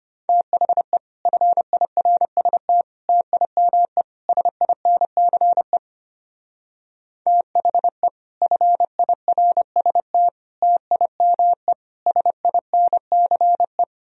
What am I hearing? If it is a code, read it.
Morse code, heard as a keyed tone, about 31 wpm, 700 Hz. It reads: THE FIRST TIME SINCE THE FIRST TIME SINCE
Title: Speed-Racing - Sets of 4 Words 25wpm